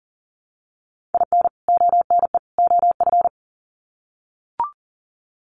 Morse code, heard as a keyed tone, about 40 wpm, 700 Hz. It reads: IN ONE OF
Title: Sets of 3 Words 40wpm